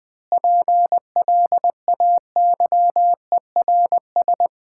Morse code, heard as a keyed tone, about 21 wpm, 700 Hz. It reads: PLAYERS
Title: Sentences with 7-Word Phrases - 20wpm